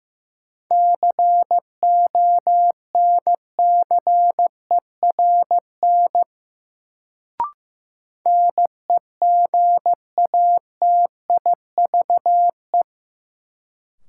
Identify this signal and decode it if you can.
Morse code, heard as a keyed tone, about 15 wpm, 700 Hz. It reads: CONCERN NEGATIVE
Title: Sentences from Top 1000 Words - New Word Drill - 15wpm